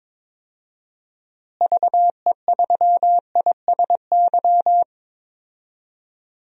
Morse code, heard as a keyed tone, about 22 wpm, 700 Hz. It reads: VE3ISY
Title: DXCC-100 Call Signs - 6 Characters - Repeated 2x - 22wpm